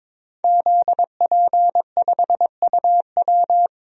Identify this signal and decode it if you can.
Morse code, heard as a keyed tone, about 22 wpm, 700 Hz. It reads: ZP5UW
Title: DXCC-100 Call Signs - 5 Characters - Repeated 2x - 22wpm